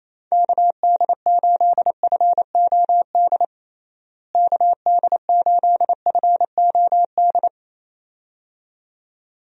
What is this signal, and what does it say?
Morse code, heard as a keyed tone, about 28 wpm, 700 Hz. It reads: KD8FOD KD8FOD
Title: USA Call Signs - 6 Characters - Repeated 2x - 28wpm